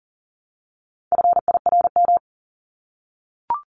Morse code, heard as a keyed tone, about 40 wpm, 700 Hz. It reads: FIRM
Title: Top 1251-1500 Words 40wpm